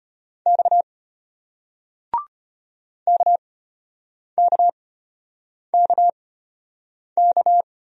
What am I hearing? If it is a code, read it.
Morse code, heard as a keyed tone, about 34 wpm, spaced as for 8 wpm, 700 Hz. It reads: X KKKK